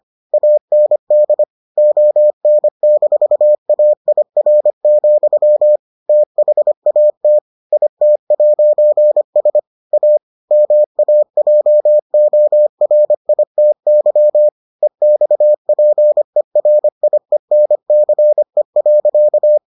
Morse code, heard as a keyed tone, about 25 wpm, 600 Hz. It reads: AND ON-AIR, THAT IT'S A MAJORITY EXPERIENCE.